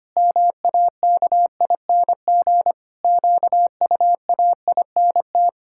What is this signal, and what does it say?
Morse code, heard as a keyed tone, about 25 wpm, 700 Hz. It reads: MAKING QUAINT
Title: Pangram Challenge v2 - 25wpm